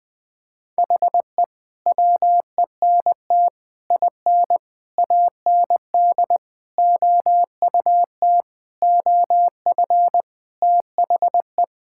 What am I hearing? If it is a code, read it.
Morse code, heard as a keyed tone, about 20 wpm, 700 Hz. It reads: HE WENT IN AND OUT OF THE
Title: Sentences with 5-Word Phrases - 20wpm